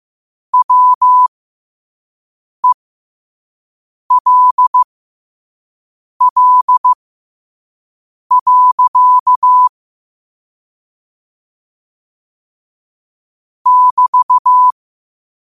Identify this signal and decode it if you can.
Morse code, heard as a keyed tone, about 15 wpm, spaced as for 5 wpm, 1000 Hz. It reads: WELL. <BT>
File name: qotdfiles / 5wpm